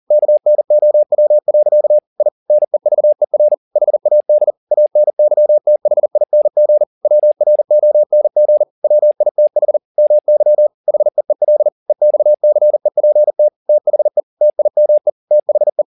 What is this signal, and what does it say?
Morse code, heard as a keyed tone, about 40 wpm, 600 Hz. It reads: KNOW. I NEVER HAD ANYTHING WRONG WITH MY HEEL EXCEPT THE TIME THE
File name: the old man and the sea 2 40wpm